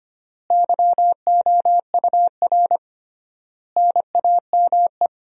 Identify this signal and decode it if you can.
Morse code, heard as a keyed tone, about 25 wpm, 700 Hz. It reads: YOUR NAME
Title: 3x Word-Spacing - Sentences from Top 200 Words - 25wpm